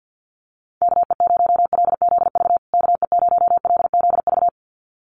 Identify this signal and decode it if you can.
Morse code, heard as a keyed tone, about 48 wpm, 700 Hz. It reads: KE0LZV KE0LZV
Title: USA Call Signs Repeated 2x 50wpm